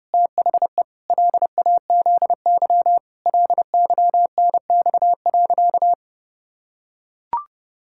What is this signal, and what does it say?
Morse code, heard as a keyed tone, about 32 wpm, 700 Hz. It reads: THE LAZY LYNX.
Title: Pangram Challenge v2 - 30wpm